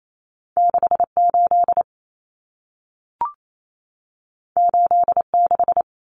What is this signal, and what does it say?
Morse code, heard as a keyed tone, about 28 wpm, 700 Hz. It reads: 68 86
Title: Common Cut Numbers - 2-Digit 28wpm